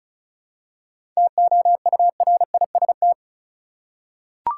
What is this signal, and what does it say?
Morse code, heard as a keyed tone, about 36 wpm, 700 Hz. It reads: TOURIST
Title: Seven Letter Words - 35wpm